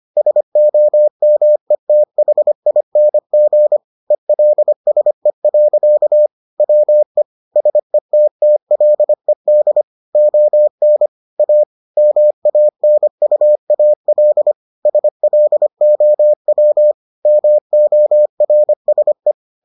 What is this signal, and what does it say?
Morse code, heard as a keyed tone, about 25 wpm, 600 Hz. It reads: SOMETHING ELSE. WE SETTLED ON A MANUAL SLOW MORSE